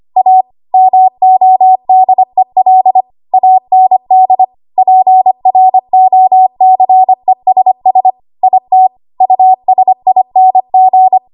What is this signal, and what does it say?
Morse code, heard as a keyed tone, about 25 wpm, 750 Hz. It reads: A MODEL AND PROCESS IT USING
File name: Archive / 25 WPM / 25WPM